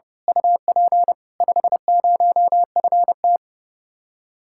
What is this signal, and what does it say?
Morse code, heard as a keyed tone, about 30 wpm, 700 Hz. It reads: UP 50FT
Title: Speed-Racing Common QSO Phrases Encore - 30wpm